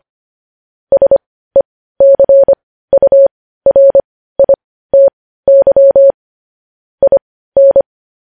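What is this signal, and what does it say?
Morse code, heard as a keyed tone, about 25 wpm, spaced as for 15 wpm, 550 Hz. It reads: SECURITY IN
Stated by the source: News headlines sent in Morse code at 15 wpm.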